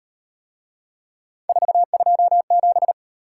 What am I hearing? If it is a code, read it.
Morse code, heard as a keyed tone, about 38 wpm, 700 Hz. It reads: 427